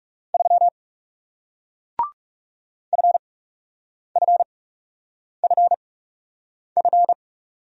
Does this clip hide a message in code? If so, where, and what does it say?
Yes: Morse code from 0.3 to 7.1 s, heard as a keyed tone, about 39 wpm, spaced as for 8 wpm, 700 Hz. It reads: 3 FFFF